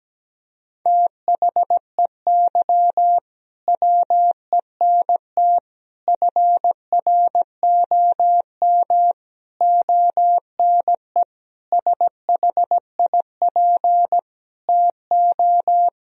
Morse code, heard as a keyed tone, about 17 wpm, 700 Hz. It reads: THEY WENT FROM ONE SHIP TO
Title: Sentences from Top 1000 Words - Review Part 2 - 17wpm